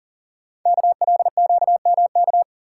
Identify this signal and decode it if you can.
Morse code, heard as a keyed tone, about 41 wpm, 690 Hz. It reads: KLQMK
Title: Code Groups - 5 Letters - 40wpm